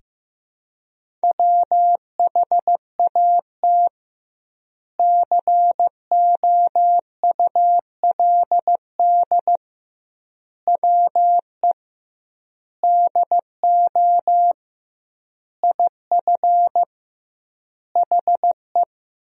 Morse code, heard as a keyed tone, about 15 wpm, 700 Hz. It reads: WHAT COULD WE DO IF HE
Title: 2x Word-Spacing - Sentences from Top 400 Words - 15wpm